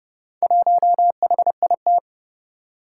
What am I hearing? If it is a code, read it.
Morse code, heard as a keyed tone, about 29 wpm, 700 Hz. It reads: 1HIT
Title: Four Challenging Letter-Number Combinations 30wpm